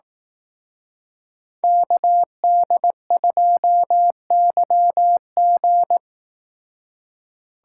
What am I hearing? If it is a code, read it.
Morse code, heard as a keyed tone, about 18 wpm, 700 Hz. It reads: KD2YG
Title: Speed Racing - USA 2x2 Call Signs 15wpm